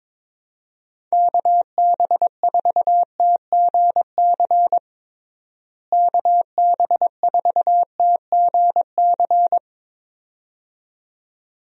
Morse code, heard as a keyed tone, about 22 wpm, 700 Hz. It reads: KB4TGC KB4TGC
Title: USA Call Signs - 6 Characters - Repeated 2x - 22wpm